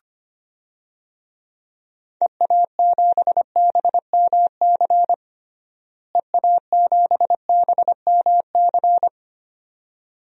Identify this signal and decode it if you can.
Morse code, heard as a keyed tone, about 25 wpm, 700 Hz. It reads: EA7BMC EA7BMC
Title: DXCC-100 Call Signs - 6 Characters - Repeated 2x - 25wpm